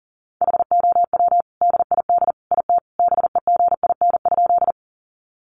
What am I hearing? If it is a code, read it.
Morse code, heard as a keyed tone, about 40 wpm, 700 Hz. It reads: HOW DID IT BEGIN?
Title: Sentences from Top 300 Words 40wpm